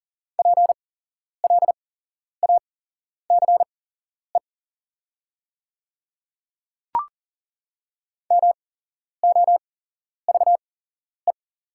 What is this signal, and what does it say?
Morse code, heard as a keyed tone, about 40 wpm, spaced as for 11 wpm, 690 Hz. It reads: PLACE MOVE
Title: 8x Character Spacing - Top 500 Words - 40wpm